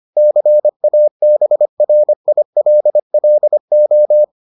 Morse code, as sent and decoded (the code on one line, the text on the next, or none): -.-. .- -... .-. .. .-.. .-.. ---
CABRILLO